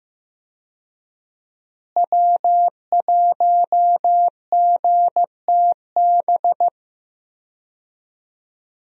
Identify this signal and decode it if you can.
Morse code, heard as a keyed tone, about 15 wpm, 700 Hz. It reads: W1GTB
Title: USA Call Signs - 5 Characters - Repeated 2x - 15wpm